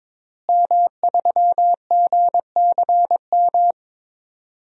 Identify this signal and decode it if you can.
Morse code, heard as a keyed tone, about 22 wpm, 700 Hz. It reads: M3GCM
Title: DXCC-100 Call Signs - 5 Characters - Repeated 2x - 22wpm